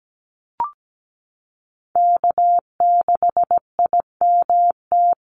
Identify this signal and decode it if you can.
Morse code, heard as a keyed tone, about 17 wpm, 700 Hz. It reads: K6IMT
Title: USA Call Signs Repeated 3x 17wpm